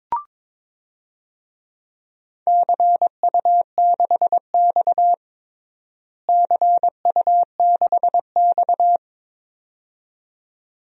Morse code, heard as a keyed tone, about 21 wpm, 700 Hz. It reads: CU6X CU6X
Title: DXCC-100 Call Signs - 4 Characters - Repeated 2x - 22wpm